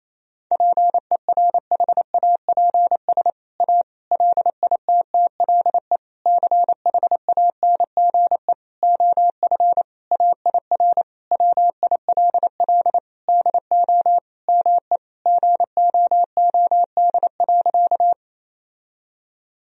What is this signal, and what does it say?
Morse code, heard as a keyed tone, about 28 wpm, 700 Hz. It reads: PERHAPS A LITTLE CHANGE OF AIR WILL DO ME GOOD.
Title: Sentences from Top 1000 Words - Review Part 1 - 28wpm